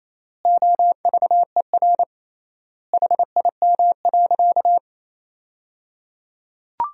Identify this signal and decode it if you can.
Morse code, heard as a keyed tone, about 28 wpm, 700 Hz. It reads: OVER HIM.
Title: 3x Word-Spacing - Sentences from Top 300 Words - 28wpm